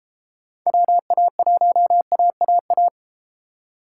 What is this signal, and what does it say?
Morse code, heard as a keyed tone, about 31 wpm, 700 Hz. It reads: WA1AAA